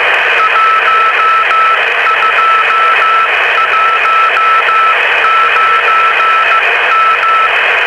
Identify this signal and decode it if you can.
Morse code, heard as a keyed tone, about 15 wpm, 1310 Hz, with radio noise. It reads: 1219M